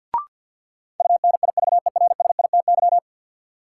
Morse code, heard as a keyed tone, about 53 wpm, 700 Hz. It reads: UNIVERSITY
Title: Sentences from Top 1000 Words - Review Part 4 - New Word Drill - 50wpm